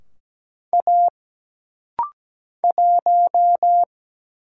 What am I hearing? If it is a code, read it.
Morse code, heard as a keyed tone, about 17 wpm, 700 Hz. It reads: A 1
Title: Single Letter-Number Alt - Rapid-Fire 17wpm